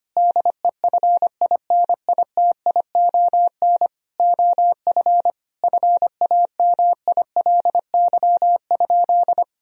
Morse code, heard as a keyed tone, about 25 wpm, 700 Hz. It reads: DEFINITION OF FAMILY?
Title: Sentences from Top 1000 Words - Review Part 4 - 25wpm